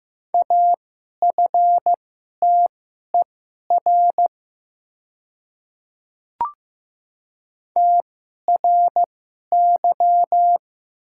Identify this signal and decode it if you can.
Morse code, heard as a keyed tone, about 15 wpm, spaced as for 11 wpm, 700 Hz. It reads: AFTER TRY
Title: 2x Character Spacing - Top 200 Words - 15wpm